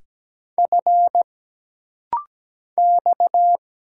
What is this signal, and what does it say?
Morse code, heard as a keyed tone, about 17 wpm, 700 Hz. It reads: F X
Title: Single Letter-Number Alt - Rapid-Fire 17wpm